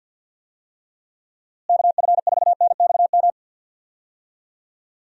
Morse code, heard as a keyed tone, about 49 wpm, 700 Hz. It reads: KF4NXM
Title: USA Call Signs - 6 Characters - Repeated 2x - 50wpm